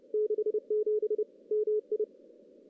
Morse code, heard as a keyed tone, about 29 wpm, 410 Hz. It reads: 67 MI
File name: contests / CW-SS-k5tr- -1500